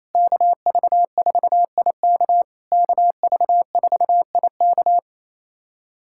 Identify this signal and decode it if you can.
Morse code, heard as a keyed tone, about 28 wpm, 690 Hz. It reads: KV4IK KV4IK
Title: SOTA QSO Practice 28wpm